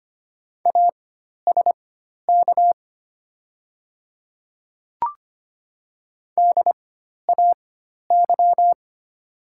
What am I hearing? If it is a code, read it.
Morse code, heard as a keyed tone, about 25 wpm, spaced as for 12 wpm, 700 Hz. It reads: ASK DAY
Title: 4x Character Spacing - Top 200 Words - 25wpm